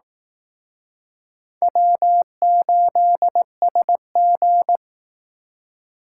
Morse code, heard as a keyed tone, about 19 wpm, 700 Hz. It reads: W8SG